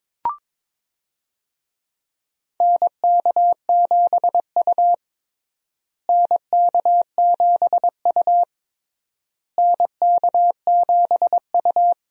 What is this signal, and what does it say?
Morse code, heard as a keyed tone, about 22 wpm, 700 Hz. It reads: NK7U NK7U NK7U
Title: USA Call Signs - 4 Characters - Repeated 3x - 22wpm